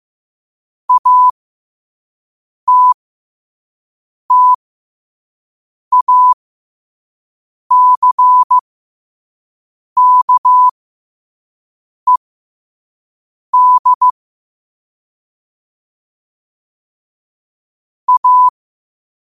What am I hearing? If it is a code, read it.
Morse code, heard as a keyed tone, about 15 wpm, spaced as for 5 wpm, 1000 Hz. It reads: ATTACKED A
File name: qotdfiles / 5wpm